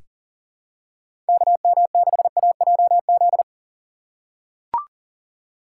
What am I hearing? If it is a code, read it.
Morse code, heard as a keyed tone, about 40 wpm, 700 Hz. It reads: KM6AJZ
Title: USA Call Signs - 3x Character Spacing - 40wpm